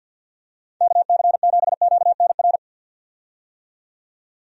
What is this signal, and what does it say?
Morse code, heard as a keyed tone, about 47 wpm, 700 Hz. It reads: KC7QNR
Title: USA Call Signs - 6 Characters - 50wpm